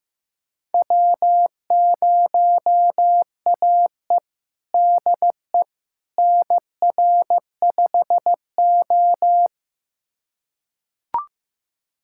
Morse code, heard as a keyed tone, about 16 wpm, 700 Hz. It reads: W0AE DE NR5O
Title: USA Callsign DE Callsign - 4 Characters - 15wpm